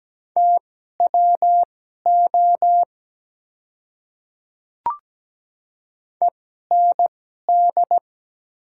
Morse code, heard as a keyed tone, about 17 wpm, spaced as for 12 wpm, 700 Hz. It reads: TWO END
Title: 2x Character Spacing - Top 300 Words - 17wpm